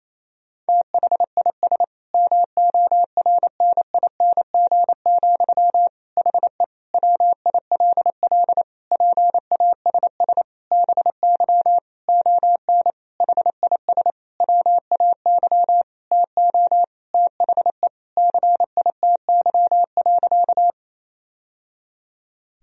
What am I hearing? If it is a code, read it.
Morse code, heard as a keyed tone, about 28 wpm, 700 Hz. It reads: THIS MORNING, HE WILL PASS BY ON HIS WAY TO THE CITY.
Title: Sentences from Top 1000 Words - Review Part 1 - 28wpm